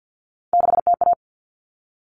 Morse code, heard as a keyed tone, about 50 wpm, 700 Hz. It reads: 6TA
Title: All Cut Numbers - 3-Digit 50wpm